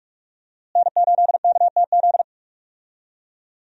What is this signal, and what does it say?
Morse code, heard as a keyed tone, about 42 wpm, 700 Hz. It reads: N8KTZ
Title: USA Call Signs - 5 Characters - Repeated 2x - 45wpm